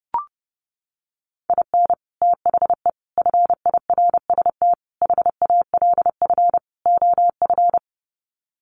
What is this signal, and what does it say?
Morse code, heard as a keyed tone, about 30 wpm, 700 Hz. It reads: IN THE FIRST HALF OF
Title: Sets of 5 Words 30wpm